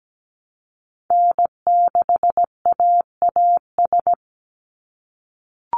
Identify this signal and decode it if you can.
Morse code, heard as a keyed tone, about 17 wpm, 700 Hz. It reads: N6AAS
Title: USA Call Signs 17wpm